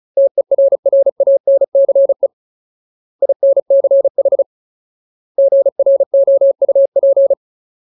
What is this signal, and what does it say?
Morse code, heard as a keyed tone, about 34 wpm, 550 Hz. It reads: TERRANCE INCH GROUP
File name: Ultimate Word List / 35WPM Ultimate